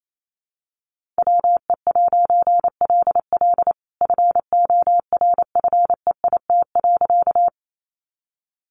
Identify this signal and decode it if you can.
Morse code, heard as a keyed tone, about 28 wpm, 700 Hz. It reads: WE'LL FORFEIT.
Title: A Sound Of Thunder 28wpm